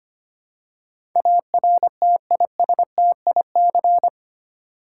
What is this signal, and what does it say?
Morse code, heard as a keyed tone, about 25 wpm, 700 Hz. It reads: ARTISTIC
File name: 8 Letter Words - First 3 Letters Given - 25wpm